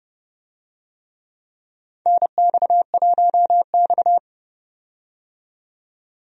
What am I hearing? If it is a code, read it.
Morse code, heard as a keyed tone, about 30 wpm, 700 Hz. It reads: NX1X
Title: USA Call Signs - 4 Characters - 30wpm